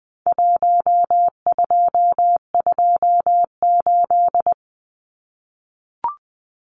Morse code, heard as a keyed tone, about 20 wpm, 700 Hz. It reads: 1228
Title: Numbers - Contest Style - 4-Digit 20wpm